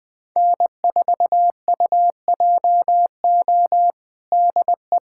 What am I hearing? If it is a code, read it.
Morse code, heard as a keyed tone, about 20 wpm, 700 Hz. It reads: N4UJO DE
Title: USA Callsign DE Callsign - 5 Characters - 20wpm